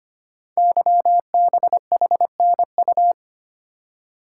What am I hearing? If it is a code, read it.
Morse code, heard as a keyed tone, about 25 wpm, 700 Hz. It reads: YBHNU
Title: Code Groups - 5 Letter:Numbers - 25wpm